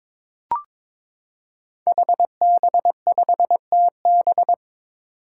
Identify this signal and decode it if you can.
Morse code, heard as a keyed tone, about 22 wpm, 700 Hz. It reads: HB5TB
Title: Five Challenging Letter-Number Combinations 22wpm